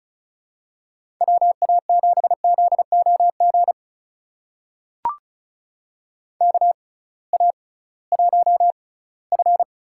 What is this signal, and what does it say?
Morse code, heard as a keyed tone, about 36 wpm, spaced as for 20 wpm, 700 Hz. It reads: WA7ZOG KA1F
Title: USA Call Signs - 6 Characters - 6x Character Spacing - 35wpm